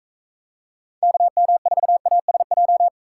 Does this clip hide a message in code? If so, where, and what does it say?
Yes: Morse code from 1.0 to 2.9 s, heard as a keyed tone, about 42 wpm, 700 Hz. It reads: KM4ASJ